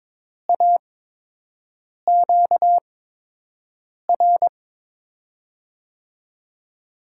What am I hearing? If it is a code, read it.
Morse code, heard as a keyed tone, about 22 wpm, spaced as for 6 wpm, 700 Hz. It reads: AQR
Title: USA Call Signs - 5 Characters - 8x Character Spacing - 22wpm